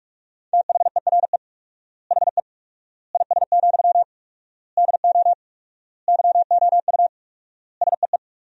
Morse code, heard as a keyed tone, about 46 wpm, 700 Hz. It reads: THERE HE IS, DO YOU SEE
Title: 4x Word-Spacing - Sentences from Top 100 Words - 45wpm